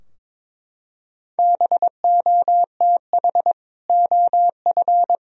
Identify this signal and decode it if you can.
Morse code, heard as a keyed tone, about 21 wpm, 700 Hz. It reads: BOTH OF
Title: Sentences from Top 1000 Words - Review Part 2 - 22wpm